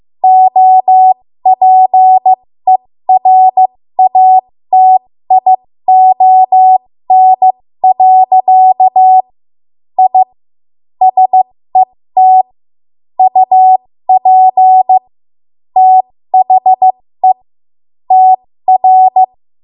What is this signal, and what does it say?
Morse code, heard as a keyed tone, about 15 wpm, spaced as for 13 wpm, 750 Hz. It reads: OPERATION. I SET UP THE TR
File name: Morse / 13WPM